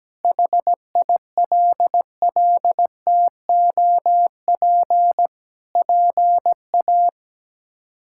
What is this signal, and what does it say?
Morse code, heard as a keyed tone, about 17 wpm, 700 Hz. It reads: HILLTOP PA
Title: K1USN SST QSO Practice - 17wpm